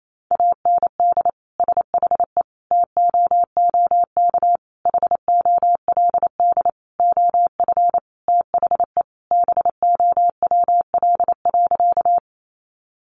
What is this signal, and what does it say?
Morse code, heard as a keyed tone, about 28 wpm, 700 Hz. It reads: AND SHE TOOK HOLD OF THE BOWL.
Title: Sentences from Top 1200 Words 28wpm